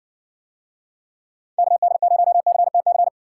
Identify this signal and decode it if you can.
Morse code, heard as a keyed tone, about 60 wpm, 700 Hz. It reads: KD0CTC